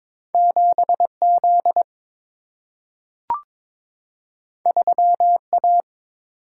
Morse code, heard as a keyed tone, about 22 wpm, 700 Hz. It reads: 7 Z 3 A
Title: Single-Letter Single-Number 22wpm